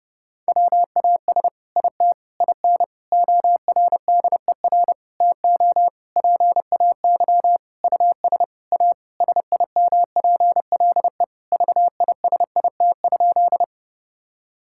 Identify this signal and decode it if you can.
Morse code, heard as a keyed tone, about 32 wpm, 700 Hz. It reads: WAS IT IN ORDER TO PAY US A SIMPLE VISIT?
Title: Sentences from Top 1000 Words - Review Part 6 - 30wpm